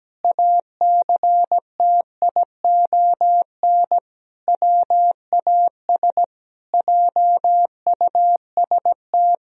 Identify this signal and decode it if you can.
Morse code, heard as a keyed tone, about 17 wpm, 700 Hz. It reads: ACTION WAS JUST
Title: Sentences from Top 1000 Words - Review Part 4 - 17wpm